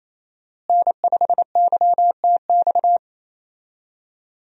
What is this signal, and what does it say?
Morse code, heard as a keyed tone, about 28 wpm, 700 Hz. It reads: N5YTX
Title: USA Call Signs - 5 Characters - Repeated 3x - 28wpm